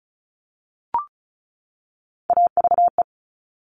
Morse code, heard as a keyed tone, about 36 wpm, 700 Hz. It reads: AVE
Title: Top Three Letter Combinations in Words 35wpm